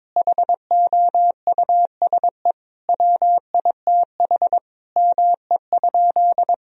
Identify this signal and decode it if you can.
Morse code, heard as a keyed tone, about 22 wpm, 700 Hz. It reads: HOUSE WITH ME?
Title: Sentences from Top 1000 Words - Review Part 1 - 22wpm